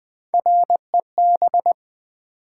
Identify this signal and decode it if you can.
Morse code, heard as a keyed tone, about 20 wpm, 700 Hz. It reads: REB
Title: Build A Word - 3 Letters At A Time - 9-Letter Words - 20wpm